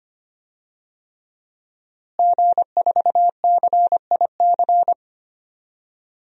Morse code, heard as a keyed tone, about 25 wpm, 700 Hz. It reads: G4CIC
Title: DXCC-100 Call Signs - 5 Characters - Repeated 3x - 25wpm